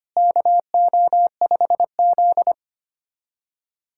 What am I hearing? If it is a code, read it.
Morse code, heard as a keyed tone, about 25 wpm, 700 Hz. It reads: KO5Z
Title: USA Callsign DE Callsign 25wpm